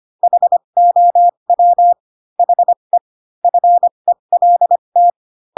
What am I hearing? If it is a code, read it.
Morse code, heard as a keyed tone, about 25 wpm, spaced as for 22 wpm, 700 Hz. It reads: HOW HE FELT
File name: Pinocchio / Pinocchio ch3 18 25 ik1hge